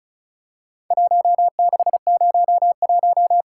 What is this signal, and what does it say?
Morse code, heard as a keyed tone, about 36 wpm, 700 Hz. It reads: 1601
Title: Worked All Europe DX Contest - QTC Exchange - 35wpm